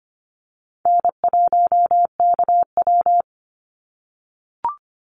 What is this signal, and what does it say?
Morse code, heard as a keyed tone, about 25 wpm, 700 Hz. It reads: N1KW
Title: Easy Call Signs - 25wpm